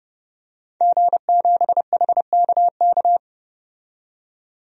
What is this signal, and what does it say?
Morse code, heard as a keyed tone, about 29 wpm, 700 Hz. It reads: G7HKK
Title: DXCC-100 Call Signs - 5 Characters - 30wpm